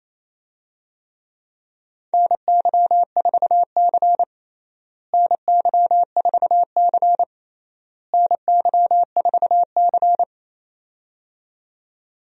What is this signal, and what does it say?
Morse code, heard as a keyed tone, about 28 wpm, 700 Hz. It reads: NY4C NY4C NY4C
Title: USA Call Signs - 4 Characters - Repeated 3x - 28wpm